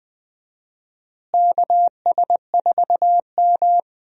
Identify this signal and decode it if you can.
Morse code, heard as a keyed tone, about 20 wpm, 700 Hz. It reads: KS4M
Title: USA Call Signs - 4 Characters - Repeated 3x - 20wpm